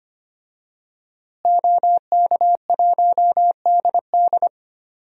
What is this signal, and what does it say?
Morse code, heard as a keyed tone, about 25 wpm, 700 Hz. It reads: OK1DD
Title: DXCC-100 Call Signs - 5 Characters - Repeated 2x - 25wpm